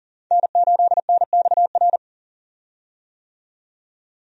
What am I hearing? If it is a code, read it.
Morse code, heard as a keyed tone, about 40 wpm, 700 Hz. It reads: N8NXR
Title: USA Call Signs - 5 Characters - 40wpm